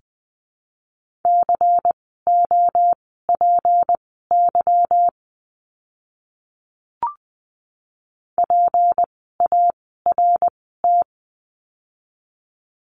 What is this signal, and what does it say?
Morse code, heard as a keyed tone, about 20 wpm, spaced as for 15 wpm, 700 Hz. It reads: COPY PART
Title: 153 Lesson 25f - Letter p - Words 20wpm